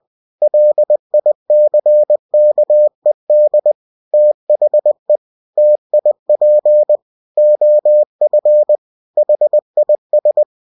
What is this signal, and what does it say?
Morse code, heard as a keyed tone, about 20 wpm, 600 Hz. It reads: LICKED THE TIP OF HIS
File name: cw-books / winnie-the-pooh / winnie-20wpm-page-0023